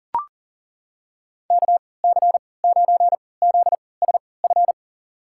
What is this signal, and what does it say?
Morse code, heard as a keyed tone, about 40 wpm, spaced as for 23 wpm, 700 Hz. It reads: KC9ZSF
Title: USA Call Signs - 6 Characters - 3x Character Spacing - 40wpm